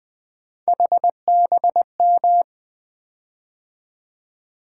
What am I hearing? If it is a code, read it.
Morse code, heard as a keyed tone, about 20 wpm, 700 Hz. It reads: HBM
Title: Three Letter-Number Combinations - 20wpm